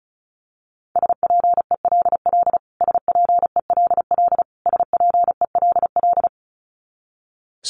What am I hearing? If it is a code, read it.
Morse code, heard as a keyed tone, about 34 wpm, 700 Hz. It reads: SPELL SPELL SPELL
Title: Top 300 Words Repeated 35wpm